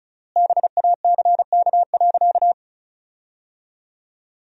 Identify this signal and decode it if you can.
Morse code, heard as a keyed tone, about 36 wpm, 700 Hz. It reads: BACK.
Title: 3x Word-Spacing - Sentences from Top 300 Words - 35wpm